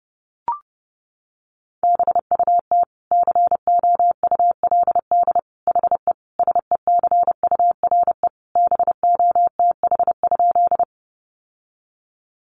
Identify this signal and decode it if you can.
Morse code, heard as a keyed tone, about 28 wpm, 700 Hz. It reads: BUT COULD HE SECURE BOTH?
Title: Sentences from Top 1200 Words 30wpm